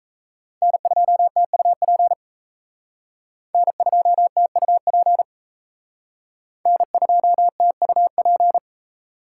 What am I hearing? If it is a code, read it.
Morse code, heard as a keyed tone, about 36 wpm, 700 Hz. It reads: N2TUP N2TUP N2TUP